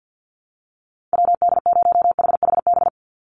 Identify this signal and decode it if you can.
Morse code, heard as a keyed tone, about 50 wpm, 700 Hz. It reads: WD0HHB
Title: USA Call Signs 50wpm